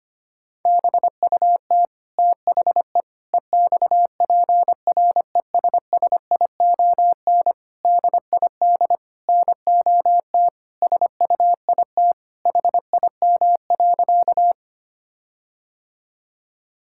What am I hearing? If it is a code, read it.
Morse code, heard as a keyed tone, about 25 wpm, 700 Hz. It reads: BUT THE EXPRESSION DID NOT SUIT HIM.
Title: Sentences from Top 1600 Words 25wpm